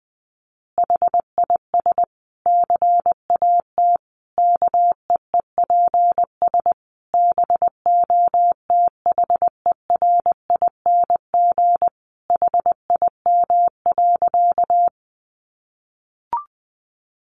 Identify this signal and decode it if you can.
Morse code, heard as a keyed tone, about 20 wpm, 700 Hz. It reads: HIS CAT KEEPS BOTHERING HIM.
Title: Additional Easy Phrases 20wpm